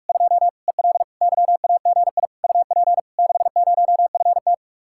Morse code, heard as a keyed tone, about 49 wpm, 700 Hz. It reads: 2 EL YAGI UP 60FT